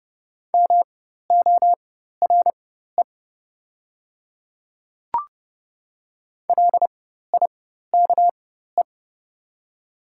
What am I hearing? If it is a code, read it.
Morse code, heard as a keyed tone, about 30 wpm, spaced as for 14 wpm, 690 Hz. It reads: MORE LIKE
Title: 4x Character Spacing - Top 100 Words - 30wpm